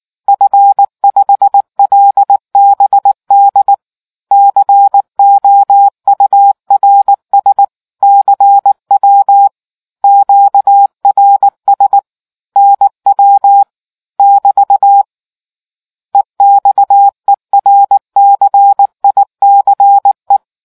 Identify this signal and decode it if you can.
Morse code, heard as a keyed tone, about 19 wpm, 800 Hz. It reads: F5LBD COURS CW QRS NW <BT> EXERCICE